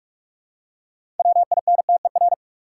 Morse code, heard as a keyed tone, about 45 wpm, 700 Hz. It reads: WINTER
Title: Six Letter Words - 45wpm